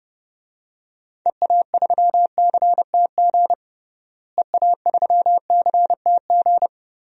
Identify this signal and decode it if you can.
Morse code, heard as a keyed tone, about 30 wpm, 700 Hz. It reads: EA3CTG EA3CTG
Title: DXCC-100 Call Signs - 6 Characters - Repeated 3x - 30wpm